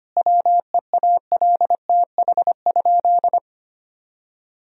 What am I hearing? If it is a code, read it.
Morse code, heard as a keyed tone, about 25 wpm, 700 Hz. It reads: WEALTH?
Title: Sentences from Top 1000 Words - Review Part 4 - 25wpm